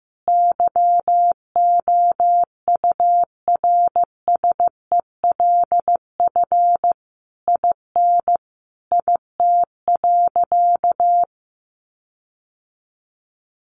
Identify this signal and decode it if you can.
Morse code, heard as a keyed tone, about 14 wpm, 700 Hz. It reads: YOURSELF IN IT.
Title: Sentences from Top 1300 Words 15wpm